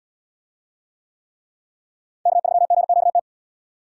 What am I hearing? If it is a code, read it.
Morse code, heard as a keyed tone, about 77 wpm, 700 Hz. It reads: K3GQT